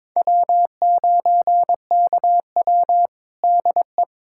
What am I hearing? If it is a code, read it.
Morse code, heard as a keyed tone, about 22 wpm, 700 Hz. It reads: W9KW DE
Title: USA Callsign DE Callsign 22wpm